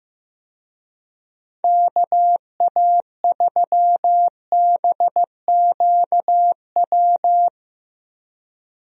Morse code, heard as a keyed tone, about 15 wpm, 700 Hz. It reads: KA3BQW
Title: USA Call Signs - 6 Characters - 15wpm